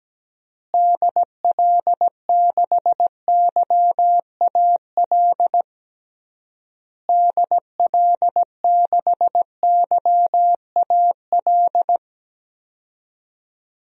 Morse code, heard as a keyed tone, about 17 wpm, 700 Hz. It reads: DL6YAL DL6YAL
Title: DXCC-100 Call Signs - 6 Characters - Repeated 3x - 17wpm